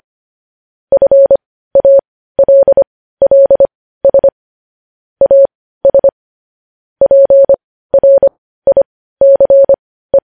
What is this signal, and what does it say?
Morse code, heard as a keyed tone, about 25 wpm, spaced as for 15 wpm, 550 Hz. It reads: FALLS AS PRICE
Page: News Headlines in Morse Code at 15 WPM